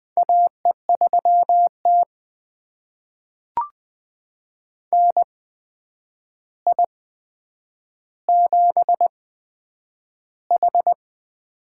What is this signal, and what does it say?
Morse code, heard as a keyed tone, about 20 wpm, spaced as for 9 wpm, 700 Hz. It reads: AE3T NI7H
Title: USA Call Signs - 4 Characters - 8x Character Spacing - 20wpm